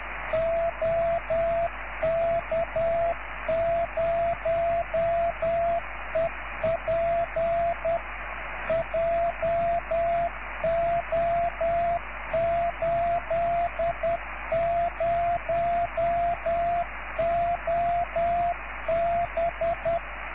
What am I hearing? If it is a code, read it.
Morse code, heard as a keyed tone, about 10 wpm, 680 Hz, with radio noise. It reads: OK0EP JO80OB